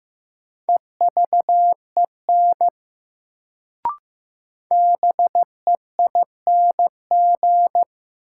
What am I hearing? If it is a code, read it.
Morse code, heard as a keyed tone, about 15 wpm, 700 Hz. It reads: EVEN BEING
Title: Sets of 4 Words - New Word Drill - 15wpm